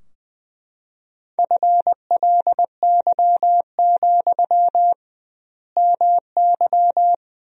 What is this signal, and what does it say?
Morse code, heard as a keyed tone, about 20 wpm, 700 Hz. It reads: FLY, MY
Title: 2x Word-Spacing - Sentences from Top 500 Words - 20wpm